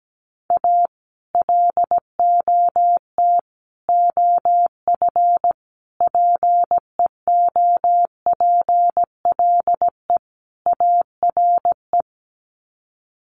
Morse code, heard as a keyed tone, about 17 wpm, 700 Hz. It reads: A LOT OF PEOPLE ARE
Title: Sets of 5 Words 17wpm